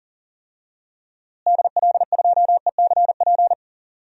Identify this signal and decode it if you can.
Morse code, heard as a keyed tone, about 40 wpm, 700 Hz. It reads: DL2ECP
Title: DXCC-100 Call Signs - 6 Characters - Repeated 3x - 40wpm